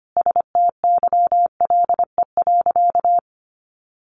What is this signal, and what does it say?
Morse code, heard as a keyed tone, about 25 wpm, 700 Hz. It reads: STYLE.
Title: Sentences from Top 800 Words 25wpm